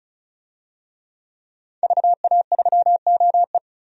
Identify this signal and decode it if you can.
Morse code, heard as a keyed tone, about 34 wpm, 700 Hz. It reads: VA3OE
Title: DXCC-100 Call Signs - 5 Characters - 35wpm